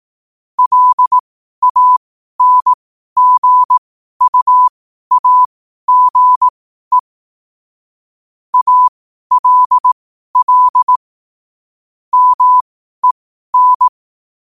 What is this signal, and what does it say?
Morse code, heard as a keyed tone, about 18 wpm, spaced as for 12 wpm, 1000 Hz. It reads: LANGUAGE ALL MEN
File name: qotdfiles / 12wpm